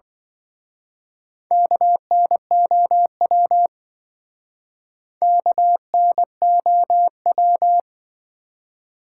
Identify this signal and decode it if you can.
Morse code, heard as a keyed tone, about 25 wpm, 700 Hz. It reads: KNOW KNOW